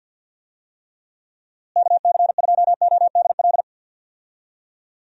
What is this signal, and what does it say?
Morse code, heard as a keyed tone, about 50 wpm, 700 Hz. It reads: KC2ODL
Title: USA Call Signs - 6 Characters - Repeated 2x - 50wpm